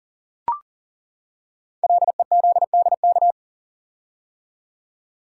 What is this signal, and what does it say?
Morse code, heard as a keyed tone, about 41 wpm, 700 Hz. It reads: LEZDK
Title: Code Groups - 5 Letters - 40wpm